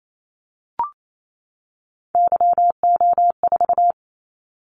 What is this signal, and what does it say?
Morse code, heard as a keyed tone, about 28 wpm, 700 Hz. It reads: YO4
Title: Callsign Prefixes for Top 25 DXCC - 28wpm